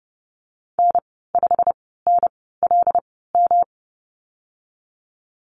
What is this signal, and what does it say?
Morse code, heard as a keyed tone, about 30 wpm, spaced as for 17 wpm, 700 Hz. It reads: N5NLM
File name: Lesson 19g - Letter w - Call Signs 30wpm